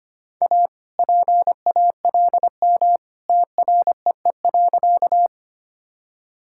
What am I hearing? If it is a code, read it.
Morse code, heard as a keyed tone, about 25 wpm, 700 Hz. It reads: A PALM TREE.
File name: practice / 25wpm / All-Age Appropriate Jokes - 25wpm